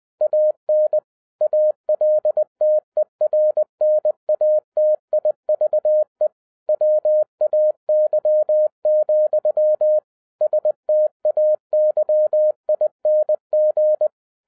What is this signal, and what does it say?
Morse code, heard as a keyed tone, about 20 wpm, 610 Hz. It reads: AN ALTERNATIVE WAY, STAYING
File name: 2011-07-mp3 / n33 THE 3.11 2 20wpm